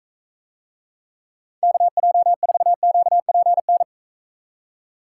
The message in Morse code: KJ4QPN

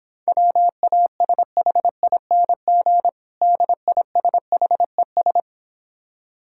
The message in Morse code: WASHING DISHES

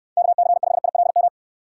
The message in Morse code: KC5ILM

